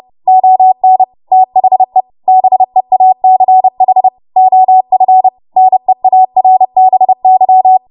Morse code, heard as a keyed tone, about 30 wpm, 750 Hz. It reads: ON THE BEACH OF NEARBY
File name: Morse / 30WPM